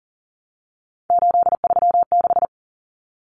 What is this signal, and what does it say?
Morse code, heard as a keyed tone, about 40 wpm, 700 Hz. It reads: 836
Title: Common Cut Numbers - 3-Digit 40wpm